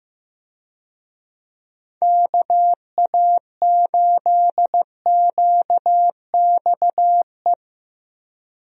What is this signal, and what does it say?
Morse code, heard as a keyed tone, about 15 wpm, 700 Hz. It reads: KA8QXE
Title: USA Call Signs - 6 Characters - Repeated 3x - 15wpm